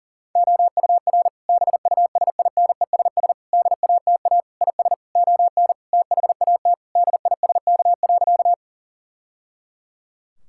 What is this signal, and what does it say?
Morse code, heard as a keyed tone, about 40 wpm, 700 Hz. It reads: OUR BUSINESS DATA IS ON THAT DISK.
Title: Sentences from Top 1000 Words - Review Part 6 - 40wpm